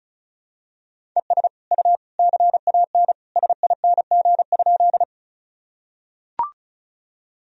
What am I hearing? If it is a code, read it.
Morse code, heard as a keyed tone, about 34 wpm, 700 Hz. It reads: ES U CAN SING?
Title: Sentences from Top 400 Words and Abbreviations 35wpm